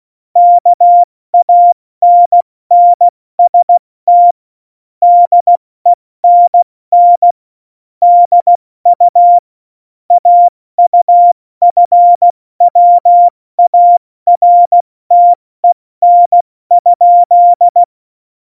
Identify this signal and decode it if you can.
Morse code, heard as a keyed tone, about 16 wpm, 700 Hz. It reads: KANNST DENN DU AUFWARTEN?
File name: Morsekurs / Witze / Witz 04 August 2020, 70 F80